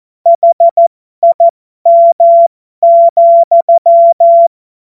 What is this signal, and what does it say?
Morse code, heard as a keyed tone, about 14 wpm, spaced as for 12 wpm, 680 Hz. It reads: HIM,